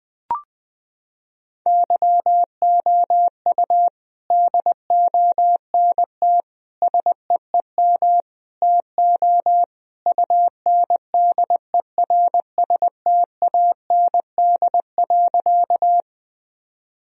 Morse code, heard as a keyed tone, about 20 wpm, 700 Hz. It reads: YOU DONT SEEM TO UNDERSTAND.
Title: Sentences from Top 1000 Words - Review Part 1 - 20wpm